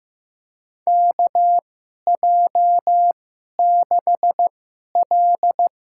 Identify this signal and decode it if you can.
Morse code, heard as a keyed tone, about 16 wpm, spaced as for 11 wpm, 700 Hz. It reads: KJ6L
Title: USA Call Signs - 2x Character Spacing - 15wpm